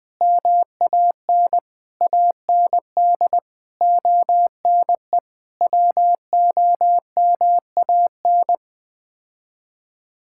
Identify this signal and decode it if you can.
Morse code, heard as a keyed tone, about 20 wpm, 700 Hz. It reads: MAN AND ONE WOMAN
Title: Sets of 6 Words - 20wpm